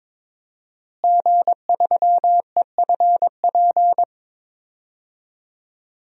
Morse code, heard as a keyed tone, about 22 wpm, 700 Hz. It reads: G3EFP
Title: DXCC-100 Call Signs - 5 Characters - Repeated 3x - 22wpm